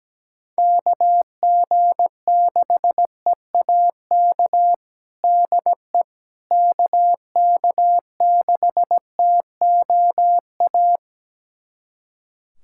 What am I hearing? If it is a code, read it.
Morse code, heard as a keyed tone, about 17 wpm, 700 Hz. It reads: KG6EAK DE KK6TOA
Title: USA Callsign DE Callsign - 6 Characters - 17wpm